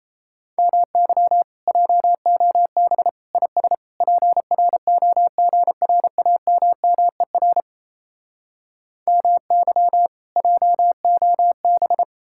Morse code, heard as a keyed tone, about 38 wpm, 700 Hz. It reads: MY JOB IS PROGRAMMER MY JOB